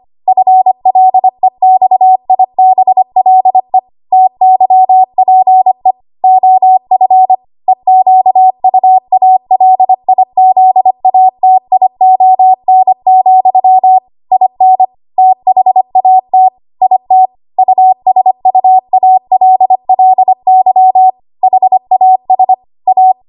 Morse code, heard as a keyed tone, about 25 wpm, 750 Hz. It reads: FLEXIBLE TYPE OF EQUALIZATION, IN THAT IT USUALLY HAS A
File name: Morse / 25WPM